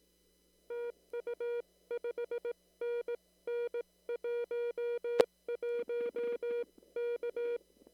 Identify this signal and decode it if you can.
Morse code, heard as a keyed tone, about 18 wpm, 480 Hz. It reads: TU 5NN 11 K
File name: CW / kt5j wpx cw-op-k5pi